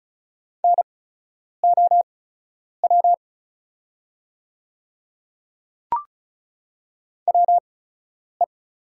Morse code, heard as a keyed tone, about 34 wpm, spaced as for 10 wpm, 690 Hz. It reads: NOW WE
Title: 8x Character Spacing - Top 100 Words - 35wpm